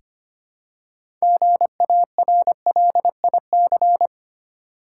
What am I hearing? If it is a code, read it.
Morse code, heard as a keyed tone, about 25 wpm, 700 Hz. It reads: GARLIC
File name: practice / 25wpm / Difficult Words with Hints - 25wpm